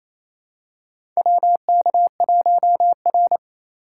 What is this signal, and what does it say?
Morse code, heard as a keyed tone, about 28 wpm, 690 Hz. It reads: WK1R
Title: USA Call Signs - 2x Character Spacing - 28wpm